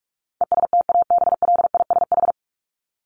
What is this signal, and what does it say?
Morse code, heard as a keyed tone, about 44 wpm, 700 Hz. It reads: ESTABLISH
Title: ED Suffix - Top 100 Words with Base Words 45wpm